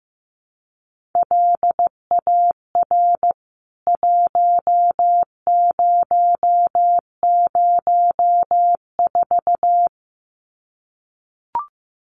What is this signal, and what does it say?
Morse code, heard as a keyed tone, about 15 wpm, 700 Hz. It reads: LAR 1004
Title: CWT Exchange 15wpm